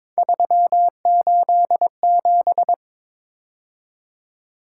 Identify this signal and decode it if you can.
Morse code, heard as a keyed tone, about 21 wpm, 700 Hz. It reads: 387
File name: practice / 22wpm / Three Letter-Number Combinations - 22wpm